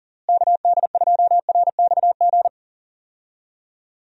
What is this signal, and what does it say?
Morse code, heard as a keyed tone, about 40 wpm, 700 Hz. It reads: KD2RXG
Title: USA Call Signs - 6 Characters - Repeated 2x - 40wpm